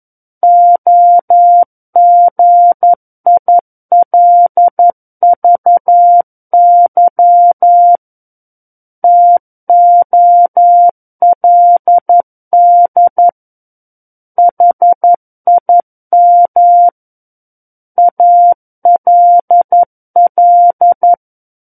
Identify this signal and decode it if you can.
Morse code, heard as a keyed tone, about 11 wpm, 700 Hz. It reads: OGILVY TOLD HIM ALL
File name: 11 WPM